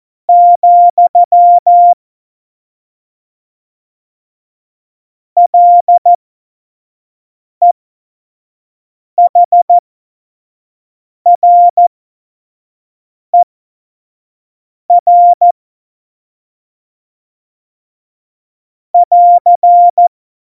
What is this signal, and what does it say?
Morse code, heard as a keyed tone, about 14 wpm, spaced as for 5 wpm, 700 Hz. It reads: , LEHRER <AR>
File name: Morsekurs / Witze / Witz 07. Mai 2019, 25 F70